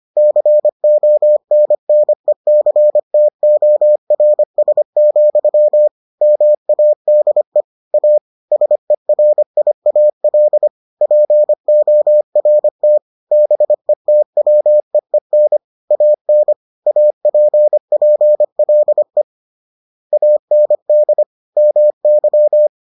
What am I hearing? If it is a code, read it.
Morse code, heard as a keyed tone, about 25 wpm, 600 Hz. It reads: CONNECTORS, MADE A SERIAL PORT BETWEEN AN APPLE AND MY